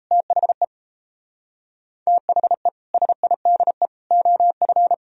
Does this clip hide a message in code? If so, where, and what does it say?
Yes: Morse code from 0.1 to 4.9 s, heard as a keyed tone, about 37 wpm, 700 Hz. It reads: THE THE SIDE OF